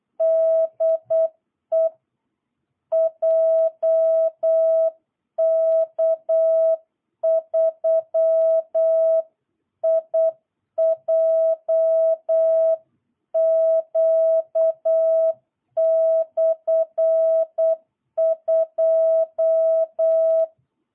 Morse code, heard as a keyed tone, about 8 wpm, 660 Hz. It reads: DE JK3IJQ/2
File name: jh8cbh / 2022cw